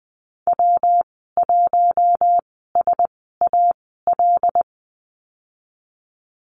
Morse code, heard as a keyed tone, about 20 wpm, spaced as for 15 wpm, 700 Hz. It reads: W1SAL
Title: 147 Lesson 24g - Letter y - Call Signs 20wpm